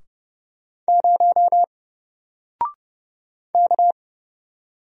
Morse code, heard as a keyed tone, about 30 wpm, 700 Hz. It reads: 0 K
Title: Single Letter-Number Alt 30wpm